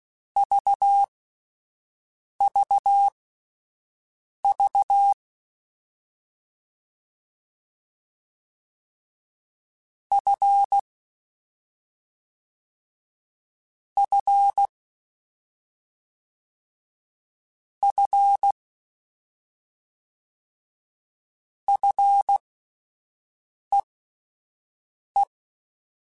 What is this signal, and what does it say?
Morse code, heard as a keyed tone, about 16 wpm, spaced as for 5 wpm, 800 Hz. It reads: VVV F F F FEE
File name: Combinations / Combinations-mp3s / Combinations@5